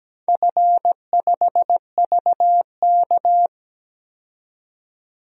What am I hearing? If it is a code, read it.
Morse code, heard as a keyed tone, about 17 wpm, 700 Hz. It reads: F5VK
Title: DXCC-100 Call Signs - 4 Characters - 17wpm